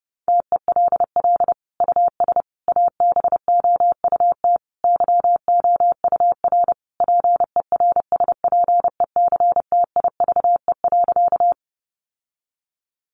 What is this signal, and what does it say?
Morse code, heard as a keyed tone, about 30 wpm, 700 Hz. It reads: TELL US ABOUT YOUR PERSPECTIVE.
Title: Sentences from Top 1200 Words 30wpm